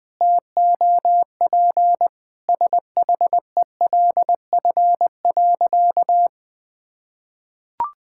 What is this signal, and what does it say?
Morse code, heard as a keyed tone, about 20 wpm, 700 Hz. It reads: TOP SHELF.
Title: Sentences with 6-Word Phrases - 20wpm